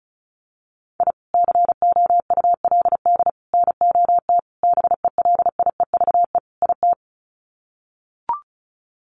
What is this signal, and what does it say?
Morse code, heard as a keyed tone, about 34 wpm, 700 Hz. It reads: I COULD NOT BELIEVE IT
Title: Sets of 5 Words Encore 35wpm